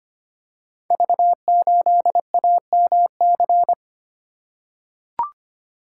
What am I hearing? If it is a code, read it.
Morse code, heard as a keyed tone, about 25 wpm, 700 Hz. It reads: V8AMC
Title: Code Groups - 5 Letter:Numbers - 25wpm